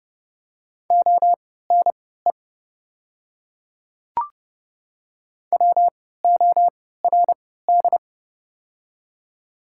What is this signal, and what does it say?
Morse code, heard as a keyed tone, about 30 wpm, spaced as for 17 wpm, 700 Hz. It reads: ONE WORD
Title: 3x Character Spacing - Top 100 Words - 30wpm